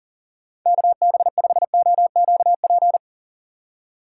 Morse code, heard as a keyed tone, about 37 wpm, 700 Hz. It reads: KB5OQP
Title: USA Call Signs - 6 Characters - 3x Character Spacing - 40wpm